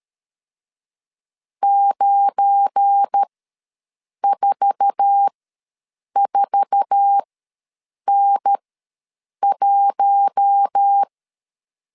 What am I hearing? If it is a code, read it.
Morse code, heard as a keyed tone, about 13 wpm, spaced as for 7 wpm, 800 Hz. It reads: 944N1
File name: LEZIONI / 8 YJ45 / 8 YJ4 ws 12wpm 3249